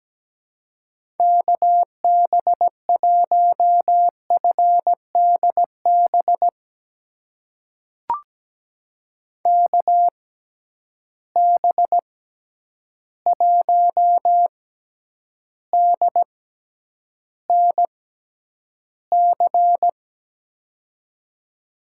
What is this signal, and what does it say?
Morse code, heard as a keyed tone, about 17 wpm, spaced as for 9 wpm, 700 Hz. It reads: KB1FDB KB1DNC
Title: USA Call Signs - 6 Characters - 6x Character Spacing - 17wpm